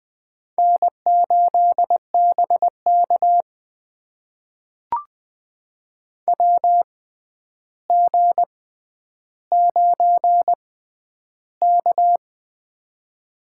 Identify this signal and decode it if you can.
Morse code, heard as a keyed tone, about 20 wpm, spaced as for 10 wpm, 700 Hz. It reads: N8BK WG9K
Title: USA Call Signs - 4 Characters - 6x Character Spacing - 20wpm